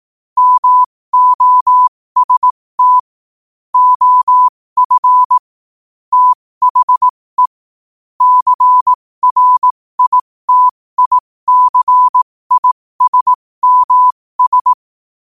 Morse code, heard as a keyed tone, about 18 wpm, spaced as for 15 wpm, 1000 Hz. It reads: MOST OF THE CRITICISMS
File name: qotdfiles / 15wpm